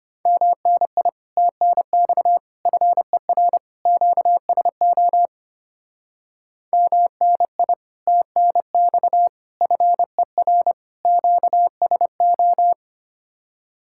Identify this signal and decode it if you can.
Morse code, heard as a keyed tone, about 29 wpm, 700 Hz. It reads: MNI TNX FER QSO MNI TNX FER QSO
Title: Speed-Racing Common QSO Phrases - 25wpm